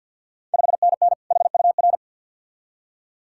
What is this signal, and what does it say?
Morse code, heard as a keyed tone, about 51 wpm, 690 Hz. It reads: 5NN HUR
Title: Ontario QSO Party Exchange 50wpm